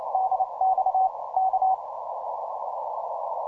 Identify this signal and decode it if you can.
Morse code, heard as a keyed tone, about 28 wpm, 760 Hz, with noise. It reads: DX K
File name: simple-file-list / NY2PO 10MCW ES